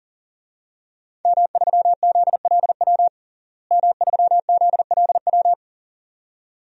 Morse code, heard as a keyed tone, about 40 wpm, 700 Hz. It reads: M3ZLW M3ZLW
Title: DXCC-100 Call Signs - 5 Characters - Repeated 2x - 40wpm